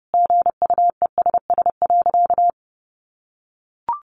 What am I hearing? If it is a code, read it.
Morse code, heard as a keyed tone, about 28 wpm, 700 Hz. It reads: GUESS.
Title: Sentences from Top 1500 Words 30wpm